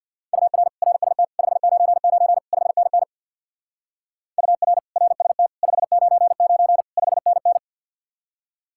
Morse code, heard as a keyed tone, about 62 wpm, 700 Hz. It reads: UR RST 599 5NN UR RST 599 5NN
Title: Speed-Racing Common QSO Phrases - 50wpm